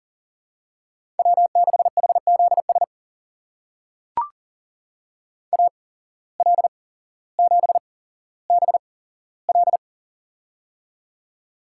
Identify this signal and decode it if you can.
Morse code, heard as a keyed tone, about 40 wpm, spaced as for 17 wpm, 700 Hz. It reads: W6HZS AL7BL
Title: USA Call Signs - 5 Characters - 8x Character Spacing - 40wpm